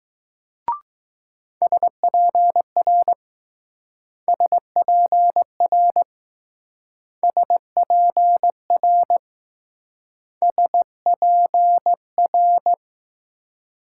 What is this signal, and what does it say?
Morse code, heard as a keyed tone, about 20 wpm, 700 Hz. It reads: SPR SPR SPR SPR